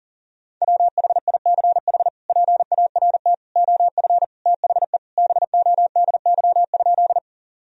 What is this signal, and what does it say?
Morse code, heard as a keyed tone, about 37 wpm, 700 Hz. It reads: WHICH PART OF THE BODY?
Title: Sentences from Top 1000 Words - Review Part 2 - 40wpm